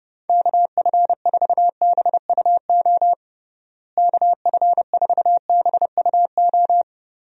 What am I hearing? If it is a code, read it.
Morse code, heard as a keyed tone, about 30 wpm, 700 Hz. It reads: KF4BUO KF4BUO
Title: USA Call Signs - 6 Characters - Repeated 3x - 30wpm